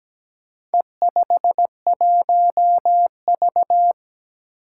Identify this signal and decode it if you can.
Morse code, heard as a keyed tone, about 17 wpm, 700 Hz. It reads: E51V
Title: Four Challenging Letter-Number Combinations 17wpm